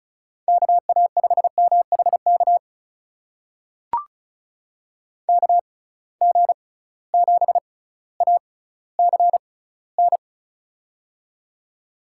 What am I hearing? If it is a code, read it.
Morse code, heard as a keyed tone, about 36 wpm, spaced as for 18 wpm, 700 Hz. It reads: KA5MHK KG7ACN
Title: USA Call Signs - 6 Characters - 6x Character Spacing - 35wpm